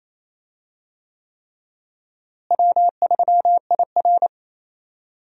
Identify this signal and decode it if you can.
Morse code, heard as a keyed tone, about 28 wpm, 700 Hz. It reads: W3IR
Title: USA Call Signs - 4 Characters - Repeated 3x - 28wpm